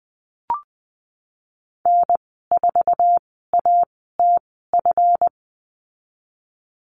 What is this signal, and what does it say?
Morse code, heard as a keyed tone, about 20 wpm, spaced as for 15 wpm, 700 Hz. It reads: N4ATF
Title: 179 Lesson 29g - Slash - Call Signs 20wpm